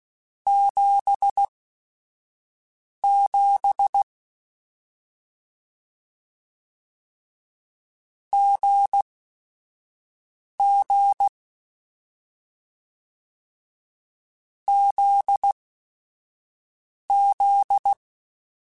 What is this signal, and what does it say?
Morse code, heard as a keyed tone, about 16 wpm, spaced as for 5 wpm, 800 Hz. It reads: 77 GG ZZ
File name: Short Runs / Short Runs-mp3s / GZ GZ7 @5